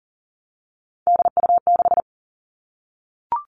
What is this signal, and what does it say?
Morse code, heard as a keyed tone, about 39 wpm, 700 Hz. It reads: DU6
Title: Three Challenging Letter-Number Combinations 40wpm